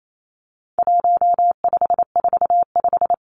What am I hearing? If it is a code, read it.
Morse code, heard as a keyed tone, about 28 wpm, 700 Hz. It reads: 1545
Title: Numbers - 4-Digit 28wpm